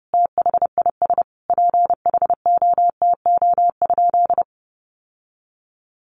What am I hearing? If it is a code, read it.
Morse code, heard as a keyed tone, about 30 wpm, 700 Hz. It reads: THIS PHOTO?
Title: Sentences from Top 1200 Words 30wpm